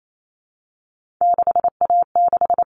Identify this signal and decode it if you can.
Morse code, heard as a keyed tone, about 28 wpm, 700 Hz. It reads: 6A6
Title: All Cut Numbers - 3-Digit 28wpm